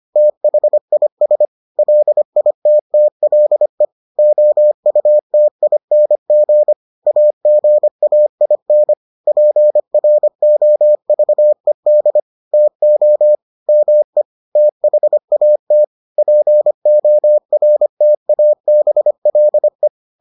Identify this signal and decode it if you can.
Morse code, heard as a keyed tone, about 25 wpm, 600 Hz. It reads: THIS LITTLE OUTING AGAIN PROVED TO ME THAT PORTABLE